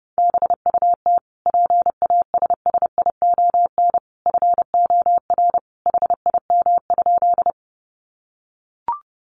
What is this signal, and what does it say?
Morse code, heard as a keyed tone, about 30 wpm, 700 Hz. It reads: BUT PASSION FOR HIM?
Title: Sentences from Top 1300 Words 30wpm